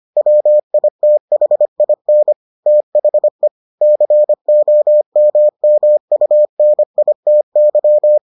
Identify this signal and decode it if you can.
Morse code, heard as a keyed tone, about 25 wpm, 600 Hz. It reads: WITHIN THE COMMUNITY